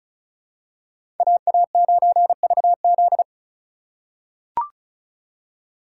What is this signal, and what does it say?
Morse code, heard as a keyed tone, about 36 wpm, 700 Hz. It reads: AA9VZ
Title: USA Call Signs - 2x Character Spacing - 35wpm